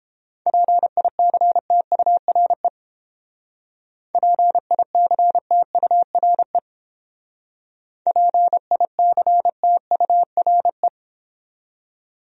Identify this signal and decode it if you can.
Morse code, heard as a keyed tone, about 30 wpm, 700 Hz. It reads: PICTURE PICTURE PICTURE